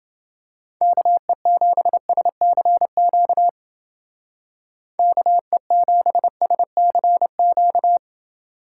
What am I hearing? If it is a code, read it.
Morse code, heard as a keyed tone, about 28 wpm, 700 Hz. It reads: KE7SCQ KE7SCQ